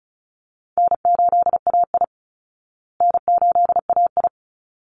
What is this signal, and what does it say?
Morse code, heard as a keyed tone, about 34 wpm, 700 Hz. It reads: N8AI N8AI
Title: Easy Call Signs Repeated 2x - 35wpm